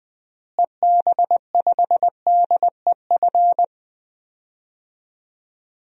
Morse code, heard as a keyed tone, about 20 wpm, 700 Hz. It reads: EB5DEF
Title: DXCC-100 Call Signs - 6 Characters - Repeated 3x - 20wpm